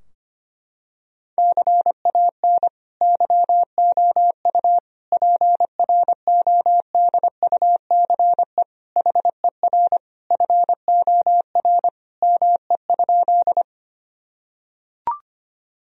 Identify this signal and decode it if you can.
Morse code, heard as a keyed tone, about 25 wpm, 700 Hz. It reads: CAN YOU PRODUCE HER FOR ME?
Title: Sentences from Top 1000 Words - Review Part 3 - 25wpm